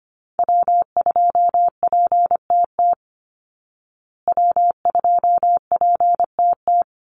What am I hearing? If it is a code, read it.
Morse code, heard as a keyed tone, about 25 wpm, 700 Hz. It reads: W2PTT W2PTT
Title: Call Signs Repeated 3x - 25wpm